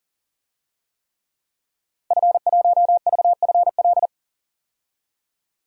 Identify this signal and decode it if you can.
Morse code, heard as a keyed tone, about 40 wpm, 700 Hz. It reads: F1VFL
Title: DXCC-100 Call Signs - 5 Characters - Repeated 3x - 40wpm